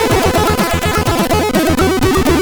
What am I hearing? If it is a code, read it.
There is no code.